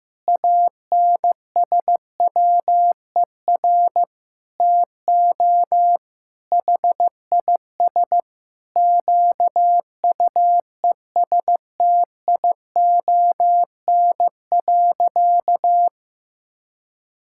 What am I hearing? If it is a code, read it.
Morse code, heard as a keyed tone, about 15 wpm, 700 Hz. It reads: ANSWER TO HIS QUESTION.
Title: Sentences from Top 1000 Words - Review Part 6 - 15wpm